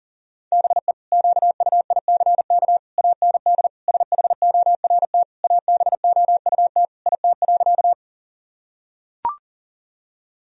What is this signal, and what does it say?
Morse code, heard as a keyed tone, about 40 wpm, 700 Hz. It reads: BE QUICK AND SHORT ABOUT IT.
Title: Sentences from Top 1000 Words - Review Part 2 - 40wpm